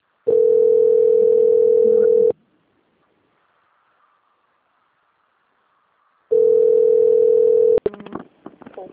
Ringback tone (North America), 440 + 480 Hz.
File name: out- - - - - .35715